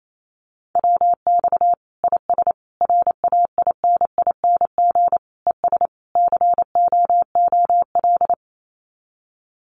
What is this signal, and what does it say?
Morse code, heard as a keyed tone, about 28 wpm, 700 Hz. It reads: WX IS RAINING ES COOL
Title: Common QSO Phrases 28wpm